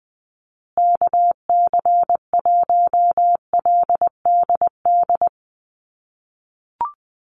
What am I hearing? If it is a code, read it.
Morse code, heard as a keyed tone, about 19 wpm, 700 Hz. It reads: KC1LDD
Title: USA Call Signs Repeated 2x 20wpm